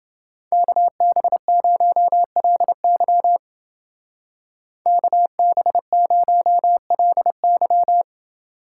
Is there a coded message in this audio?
Yes: Morse code from 0.5 to 8.0 s, heard as a keyed tone, about 29 wpm, 700 Hz. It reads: KB0LY KB0LY